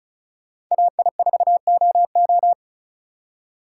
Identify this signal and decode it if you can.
Morse code, heard as a keyed tone, about 44 wpm, 700 Hz. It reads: AI4OO